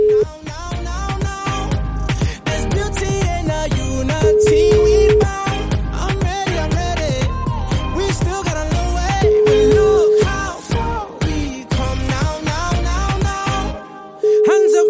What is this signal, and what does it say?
Ringback tone (Europe, 425 Hz), with noise.